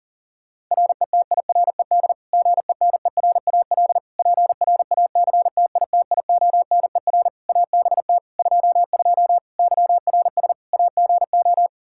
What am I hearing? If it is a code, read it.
Morse code, heard as a keyed tone, about 40 wpm, 700 Hz. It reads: RETIRED GENERAL PRACTITIONER ABT 22 YRS AGO
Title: Practice QSO - Exchange 3 - More Info - 40wpm